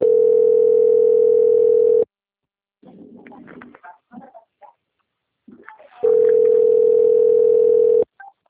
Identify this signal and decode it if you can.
Ringback tone (North America), 440 + 480 Hz.
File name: out- - - - - .86871